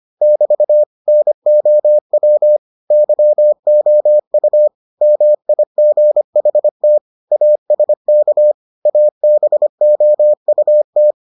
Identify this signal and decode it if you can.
Morse code, heard as a keyed tone, about 25 wpm, 600 Hz. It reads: <BT> NOW YOU MIGHT ASK ABOUT